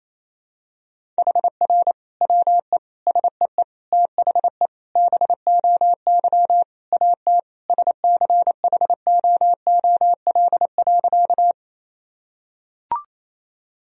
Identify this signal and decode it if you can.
Morse code, heard as a keyed tone, about 28 wpm, 700 Hz. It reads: HR WE SEE THE BOY AT SCHOOL.
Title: Sentences from Top 300 Words and Abbreviations 28wpm